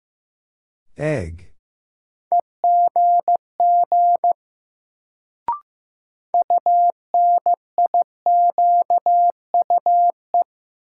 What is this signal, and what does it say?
Morse code, heard as a keyed tone, about 16 wpm, 700 Hz. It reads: EGG UNIQUE
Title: Sentences from Top 1000 Words - Review Part 7 - New Word Drill - 15wpm